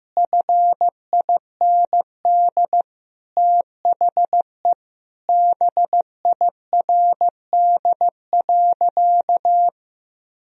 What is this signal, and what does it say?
Morse code, heard as a keyed tone, about 15 wpm, 700 Hz. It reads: FIND THE BIRD.
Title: Sentences from Top 1000 Words - Review Part 2 - 15wpm